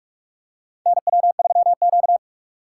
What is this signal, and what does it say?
Morse code, heard as a keyed tone, about 46 wpm, 690 Hz. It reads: NW3Q
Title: USA Call Signs - 4 Characters - 3x Character Spacing - 45wpm